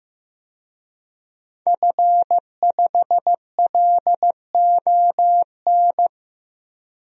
Morse code, heard as a keyed tone, about 15 wpm, 700 Hz. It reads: F5LON
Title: DXCC-100 Call Signs - 5 Characters - Repeated 2x - 15wpm